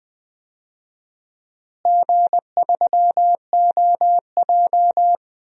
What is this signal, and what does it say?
Morse code, heard as a keyed tone, about 21 wpm, 700 Hz. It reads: G3OJ
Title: DXCC-100 Call Signs - 4 Characters - Repeated 2x - 20wpm